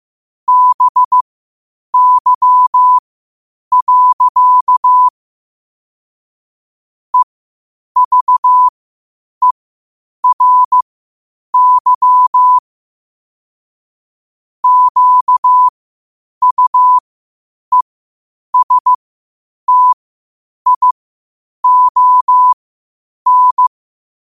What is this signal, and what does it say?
Morse code, heard as a keyed tone, about 15 wpm, spaced as for 8 wpm, 1000 Hz. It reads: BY. EVERY QUESTION
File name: qotdfiles / 8wpm